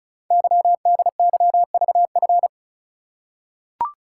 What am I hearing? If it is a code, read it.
Morse code, heard as a keyed tone, about 36 wpm, 700 Hz. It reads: YDYVF
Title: Code Groups - 5 Letters - 35wpm